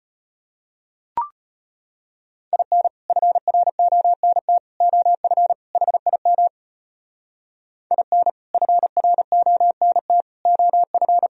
Morse code, heard as a keyed tone, about 33 wpm, 700 Hz. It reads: IN FRONT OF HIM IN FRONT OF